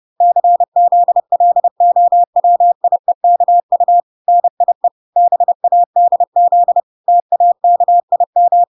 Morse code, heard as a keyed tone, about 30 wpm, 700 Hz. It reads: CZLOWIEKU NIE BADZ TAKIM